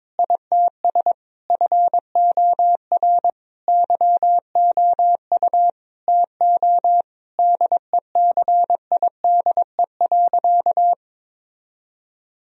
Morse code, heard as a keyed tone, about 22 wpm, 700 Hz. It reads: ITS FOR YOU TO DECIDE.
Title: Sentences from Top 1000 Words - Review Part 3 - 22wpm